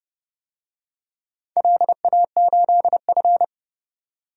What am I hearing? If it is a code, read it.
Morse code, heard as a keyed tone, about 30 wpm, 700 Hz. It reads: LA8F
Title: DXCC-100 Call Signs - 4 Characters - 30wpm